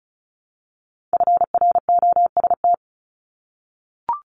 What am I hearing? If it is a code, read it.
Morse code, heard as a keyed tone, about 35 wpm, 700 Hz. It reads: FROST
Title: Multi-letter Phonemes Repeated 3x 35wpm